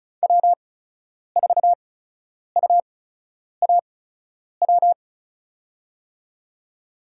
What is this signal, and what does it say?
Morse code, heard as a keyed tone, about 36 wpm, spaced as for 10 wpm, 700 Hz. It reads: W4UAW
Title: USA Call Signs - 8x Character Spacing - 35wpm